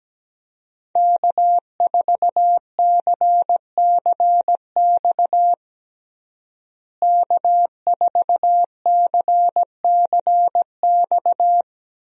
Morse code, heard as a keyed tone, about 17 wpm, 700 Hz. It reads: K4CCX K4CCX
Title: USA Call Signs - 5 Characters - Repeated 3x - 17wpm